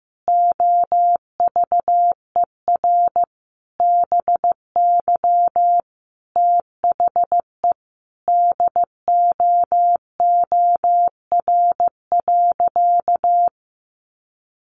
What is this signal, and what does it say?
Morse code, heard as a keyed tone, about 15 wpm, 700 Hz. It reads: OVER BY THE DOOR.
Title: Sentences from Top 1400 Words 15wpm